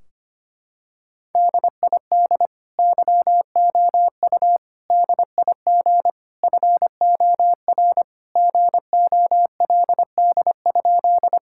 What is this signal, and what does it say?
Morse code, heard as a keyed tone, about 25 wpm, 700 Hz. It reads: DID YOU DIG FOR GOLD?
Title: Sentences from Top 1600 Words 25wpm